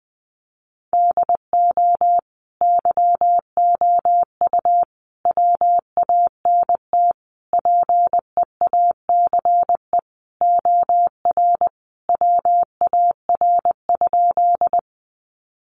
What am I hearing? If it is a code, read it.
Morse code, heard as a keyed tone, about 20 wpm, 700 Hz. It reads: DO YOU WANT PEACE OR WAR?
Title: Sentences from Top 1500 Words 20wpm